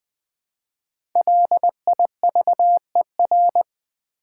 Morse code, heard as a keyed tone, about 20 wpm, 700 Hz. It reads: LIVER
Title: Build A Word - 1 Letter At A Time - 5-Letter Words - 20wpm